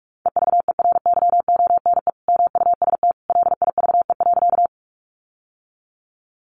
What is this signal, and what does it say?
Morse code, heard as a keyed tone, about 44 wpm, 700 Hz. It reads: EVERYONE MUST LIVE.
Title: Sentences from Top 800 Words 45wpm